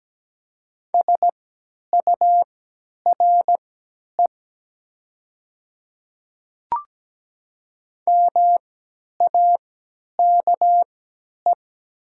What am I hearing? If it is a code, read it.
Morse code, heard as a keyed tone, about 17 wpm, spaced as for 10 wpm, 700 Hz. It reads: SURE MAKE
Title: 3x Character Spacing - Top 300 Words - 17wpm